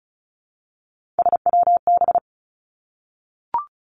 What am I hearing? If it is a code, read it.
Morse code, heard as a keyed tone, about 36 wpm, 700 Hz. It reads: SWB
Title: Three Challenging Letter-Number Combinations 35wpm